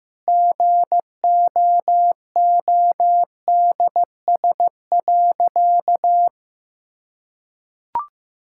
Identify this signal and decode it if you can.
Morse code, heard as a keyed tone, about 15 wpm, 700 Hz. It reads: GOODS.
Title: Sentences from Top 1000 Words - Review Part 6 - 15wpm